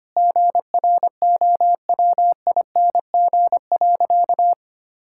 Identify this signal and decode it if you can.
Morse code, heard as a keyed tone, about 25 wpm, 700 Hz. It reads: GROWING.
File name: practice / 25wpm / Sentences with 2-Word Phrases - 25wpm